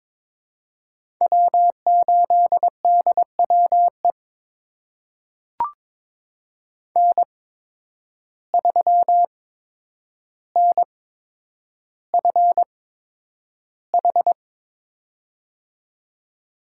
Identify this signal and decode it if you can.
Morse code, heard as a keyed tone, about 22 wpm, spaced as for 9 wpm, 700 Hz. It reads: W8DWE N3NFH
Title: USA Call Signs - 5 Characters - 8x Character Spacing - 22wpm